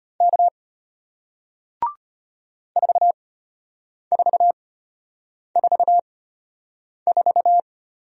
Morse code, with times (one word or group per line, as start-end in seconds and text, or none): 0.2-0.5 K
2.8-7.6 4444